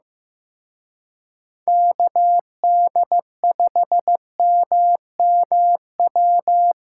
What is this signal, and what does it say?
Morse code, heard as a keyed tone, about 18 wpm, 700 Hz. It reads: KD5MMW
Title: Speed Racing - USA 2x3 Call Signs 15wpm